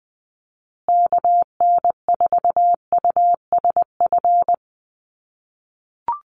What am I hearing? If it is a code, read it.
Morse code, heard as a keyed tone, about 20 wpm, 700 Hz. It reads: KN4USF
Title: USA Call Signs Repeated 2x 20wpm